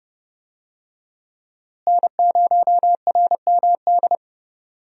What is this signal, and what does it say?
Morse code, heard as a keyed tone, about 30 wpm, 700 Hz. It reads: N0RMD
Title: USA Call Signs - 5 Characters - Repeated 3x - 30wpm